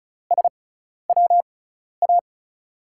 Morse code, heard as a keyed tone, about 36 wpm, spaced as for 12 wpm, 700 Hz. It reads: SWA
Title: USA Call Signs - 6x Character Spacing - 35wpm